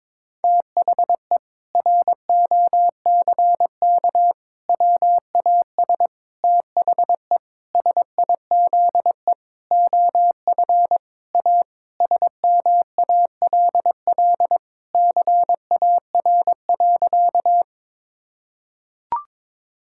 Morse code, heard as a keyed tone, about 22 wpm, 700 Hz. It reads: THE ROCK WAS THE SIZE OF A SMALL CAR.
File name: practice / 22wpm / Sentences with 4-Word Phrases - 22wpm